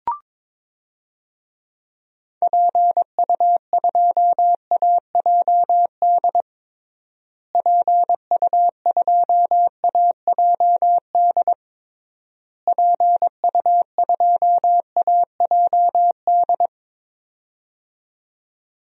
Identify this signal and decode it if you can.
Morse code, heard as a keyed tone, about 22 wpm, 700 Hz. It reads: PU2AJD PU2AJD PU2AJD
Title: DXCC-100 Call Signs - 6 Characters - Repeated 3x - 22wpm